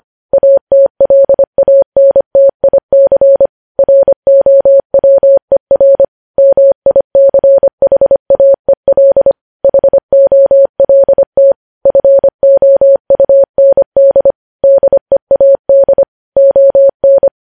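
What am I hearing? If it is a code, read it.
Morse code, heard as a keyed tone, about 25 wpm, 550 Hz. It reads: ATLANTIC ROWER MICHAEL HOLT FOUND DEAD ON
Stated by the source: Morse code news headlines at 25 wpm.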